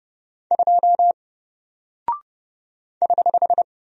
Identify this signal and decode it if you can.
Morse code, heard as a keyed tone, about 30 wpm, 700 Hz. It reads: 2 <HH>
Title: Single Character (Letter:Number:Punctuation:Prosign) Alt 30wpm